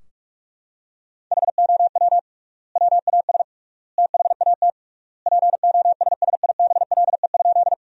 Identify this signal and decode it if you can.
Morse code, heard as a keyed tone, about 46 wpm, 700 Hz. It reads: HOW WAS THAT POSSIBLE?
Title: 3x Word-Spacing - Sentences from Top 500 Words - 45wpm